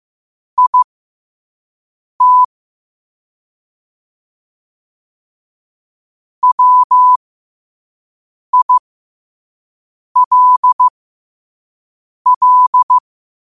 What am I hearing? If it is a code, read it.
Morse code, heard as a keyed tone, about 15 wpm, spaced as for 5 wpm, 1000 Hz. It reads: IT WILL
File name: qotdfiles / 5wpm